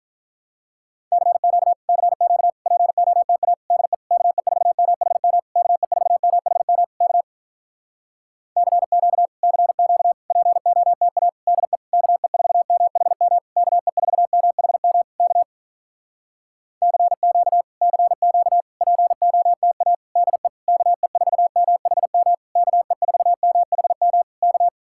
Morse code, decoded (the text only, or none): CQ CQ POTA DE KE4MHM KE4MHM K CQ CQ POTA DE KE4MHM KE4MHM K CQ CQ POTA DE KE4MHM KE4MHM K